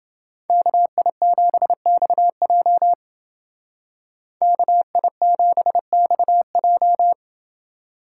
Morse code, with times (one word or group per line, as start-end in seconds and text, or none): 0.5-2.9 KI7XJ
4.4-7.1 KI7XJ